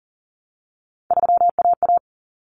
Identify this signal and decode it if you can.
Morse code, heard as a keyed tone, about 40 wpm, 700 Hz. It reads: 3AA
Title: Common Cut Numbers - 3-Digit 40wpm